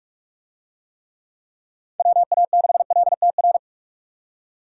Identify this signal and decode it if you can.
Morse code, heard as a keyed tone, about 42 wpm, 700 Hz. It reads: WA6LTR